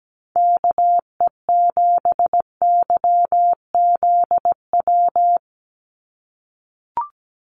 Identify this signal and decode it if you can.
Morse code, heard as a keyed tone, about 17 wpm, 700 Hz. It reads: KE7YZW
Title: USA Call Signs 17wpm